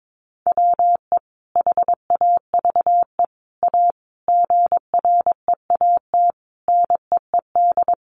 Morse code, heard as a keyed tone, about 22 wpm, 700 Hz. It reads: WE HAVE A GREAT NEED
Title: Sentences from Top 1500 Words 22wpm